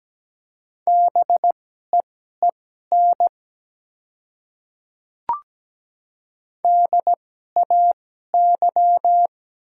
Morse code, heard as a keyed tone, about 17 wpm, spaced as for 12 wpm, 690 Hz. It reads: BEEN DAY
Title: 2x Character Spacing - Top 100 Words - 17wpm